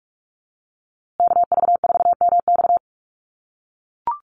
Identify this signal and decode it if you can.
Morse code, heard as a keyed tone, about 44 wpm, 700 Hz. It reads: KV4MX
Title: USA Call Signs Repeated 3x 45wpm